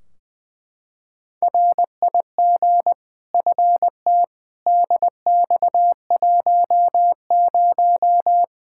Morse code, heard as a keyed tone, about 21 wpm, 700 Hz. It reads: RIG FT DX10
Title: Practice QSO - Exchange 2 - Info - 20wpm